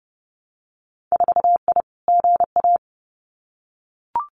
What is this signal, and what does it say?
Morse code, heard as a keyed tone, about 30 wpm, 700 Hz. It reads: 4I GA
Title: Winter Field Day Exchanges Encore 30wpm